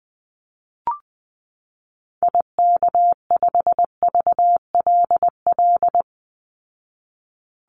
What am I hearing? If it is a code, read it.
Morse code, heard as a keyed tone, about 20 wpm, 700 Hz. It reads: IK5VLL
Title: Call Signs Repeated 2x - 20wpm